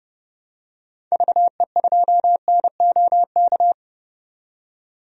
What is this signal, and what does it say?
Morse code, heard as a keyed tone, about 30 wpm, 700 Hz. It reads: VE2NOK
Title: DXCC-100 Call Signs - 6 Characters - 30wpm